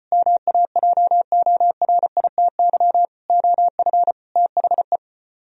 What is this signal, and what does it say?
Morse code, heard as a keyed tone, about 34 wpm, 690 Hz. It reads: MAJORITY OF THE